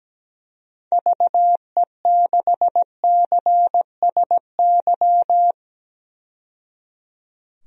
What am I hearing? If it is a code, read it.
Morse code, heard as a keyed tone, about 17 wpm, 700 Hz. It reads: VE6CSY
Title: DXCC-100 Call Signs - 6 Characters - 17wpm